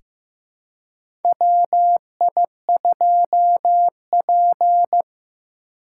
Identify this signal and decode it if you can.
Morse code, heard as a keyed tone, about 15 wpm, 700 Hz. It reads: WI2P
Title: USA Call Signs - 4 Characters - 4x Character Spacing - 15wpm